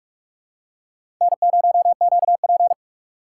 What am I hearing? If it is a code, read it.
Morse code, heard as a keyed tone, about 46 wpm, 700 Hz. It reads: N0QP
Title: USA Call Signs - 4 Characters - 3x Character Spacing - 45wpm